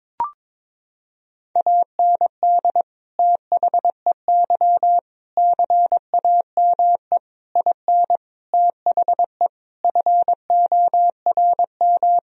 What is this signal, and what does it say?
Morse code, heard as a keyed tone, about 22 wpm, 700 Hz. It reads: AND THEY CAME IN THE FORM
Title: Sentences from Top 1000 Words - Review Part 1 - 22wpm